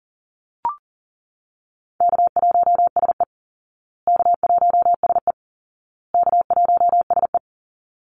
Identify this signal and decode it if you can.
Morse code, heard as a keyed tone, about 40 wpm, 700 Hz. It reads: K1SE K1SE K1SE
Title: Easy Call Signs Repeated 3x - 40wpm